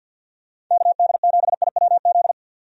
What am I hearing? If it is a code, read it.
Morse code, heard as a keyed tone, about 48 wpm, 700 Hz. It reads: KD7IWZ
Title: USA Call Signs - 6 Characters - 50wpm